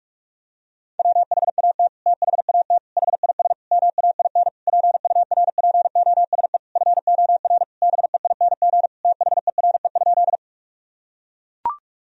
Morse code, heard as a keyed tone, about 41 wpm, 700 Hz. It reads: WHAT THAT HIS MAIN PURPOSE FOR BEING THERE?
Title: Sentences from Top 1000 Words - Review Part 2 - 45wpm